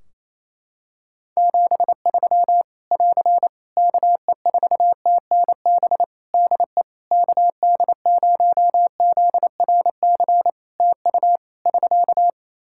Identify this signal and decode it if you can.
Morse code, heard as a keyed tone, about 27 wpm, 700 Hz. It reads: 73 <AR> KE4TNB DE KD0ZRC TU <SK>
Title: K1USN SST QSO Practice 28wpm